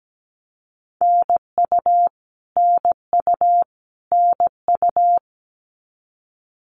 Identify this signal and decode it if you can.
Morse code, heard as a keyed tone, about 17 wpm, 700 Hz. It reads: NU NU NU
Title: Canadian Provinces and Territories - Repeated 3x - 17wpm